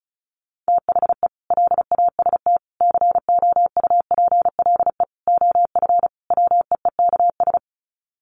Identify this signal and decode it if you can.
Morse code, heard as a keyed tone, about 34 wpm, 700 Hz. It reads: THE LAST COUPLE OF WEEKS
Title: Sets of 5 Words Encore2 35wpm